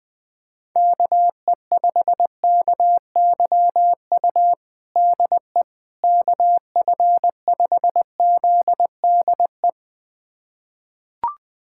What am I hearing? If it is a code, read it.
Morse code, heard as a keyed tone, about 20 wpm, 700 Hz. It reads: KE5KYU DE KF5ZDE
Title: USA Callsign DE Callsign 20wpm